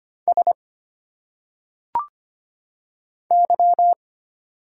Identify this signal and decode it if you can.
Morse code, heard as a keyed tone, about 25 wpm, 700 Hz. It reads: S Y
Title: Single Letters - Phonetic Pronunciation 25wpm